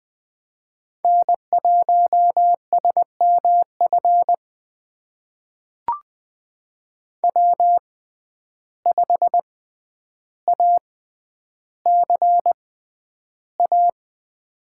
Morse code, heard as a keyed tone, about 20 wpm, spaced as for 10 wpm, 700 Hz. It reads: N1SMF W5ACA
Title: USA Call Signs - 5 Characters - 6x Character Spacing - 20wpm